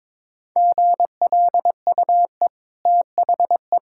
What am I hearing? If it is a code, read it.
Morse code, heard as a keyed tone, about 22 wpm, 690 Hz. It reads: GLUE THE
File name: practice / 22wpm / Easy Sentences with Hints - 22wpm